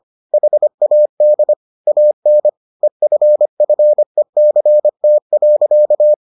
Morse code, heard as a keyed tone, about 25 wpm, 600 Hz. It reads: HAD AN EFFECT.